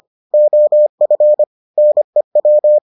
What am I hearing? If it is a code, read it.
Morse code, heard as a keyed tone, about 25 wpm, 600 Hz. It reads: OF NEW